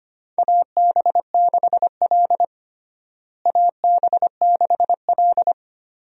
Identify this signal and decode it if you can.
Morse code, heard as a keyed tone, about 25 wpm, 700 Hz. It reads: AB6L AB6L
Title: USA Call Signs - 4 Characters - Repeated 3x - 25wpm